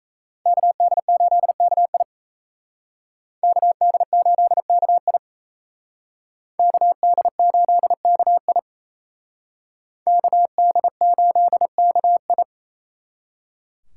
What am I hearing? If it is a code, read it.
Morse code, heard as a keyed tone, about 35 wpm, 700 Hz. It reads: KD8KI KD8KI KD8KI KD8KI